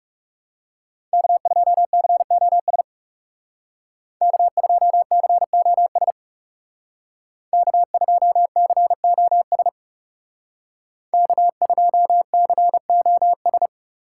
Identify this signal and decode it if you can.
Morse code, heard as a keyed tone, about 36 wpm, 700 Hz. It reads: K2COS K2COS K2COS K2COS
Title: Speed Racing - USA 1x3 Call Signs 30wpm